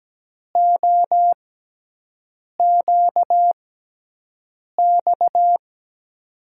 Morse code, heard as a keyed tone, about 17 wpm, spaced as for 6 wpm, 700 Hz. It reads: OQX
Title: USA Call Signs - 6 Characters - 6x Character Spacing - 17wpm